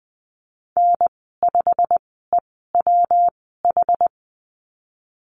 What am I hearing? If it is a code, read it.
Morse code, heard as a keyed tone, about 20 wpm, spaced as for 15 wpm, 700 Hz. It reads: N5EWH
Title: 128 Lesson 21g - Number 6 - Call Signs 20wpm